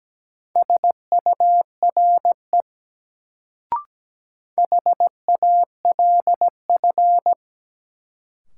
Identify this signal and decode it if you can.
Morse code, heard as a keyed tone, about 17 wpm, 700 Hz. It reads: SURE HALF
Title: Sets of 3 Words - New Word Drill - 17wpm